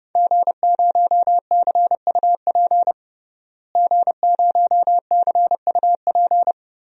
Morse code, heard as a keyed tone, about 30 wpm, 700 Hz. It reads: G0CUP G0CUP
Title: DXCC-100 Call Signs - 5 Characters - Repeated 3x - 30wpm